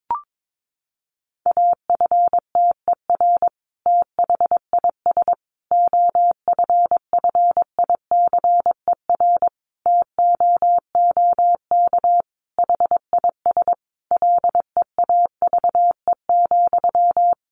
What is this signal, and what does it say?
Morse code, heard as a keyed tone, about 22 wpm, 700 Hz. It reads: AFTER THIS OFFICER TOOK HIS LEAVE,
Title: Sentences from Top 1200 Words 22wpm